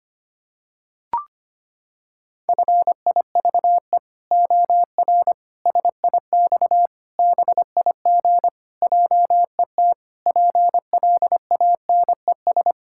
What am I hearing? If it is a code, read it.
Morse code, heard as a keyed tone, about 25 wpm, 700 Hz. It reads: FIVE OR SIX BIG JET PLANES
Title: Pangram Challenge v2 - 25wpm